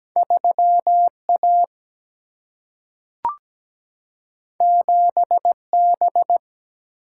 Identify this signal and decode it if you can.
Morse code, heard as a keyed tone, about 17 wpm, 700 Hz. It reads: 3 A 7 B
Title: Single-Letter Single-Number 17wpm